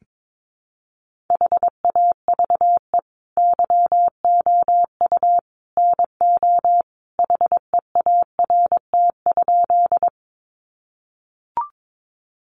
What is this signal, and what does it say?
Morse code, heard as a keyed tone, about 22 wpm, 700 Hz. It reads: HAVE YOU NO HEART?
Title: Sentences from Top 1200 Words 22wpm